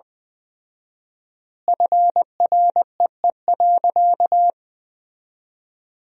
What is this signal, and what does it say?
Morse code, heard as a keyed tone, about 21 wpm, 700 Hz. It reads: FREE.
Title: 4x Word-Spacing - Sentences from Top 500 Words - 20wpm